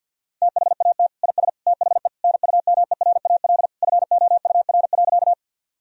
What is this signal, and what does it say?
Morse code, heard as a keyed tone, about 51 wpm, 700 Hz. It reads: THAT IS THE NUMERAL FOUR.
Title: Sentences from Top 1000 Words - Review Part 2 - 50wpm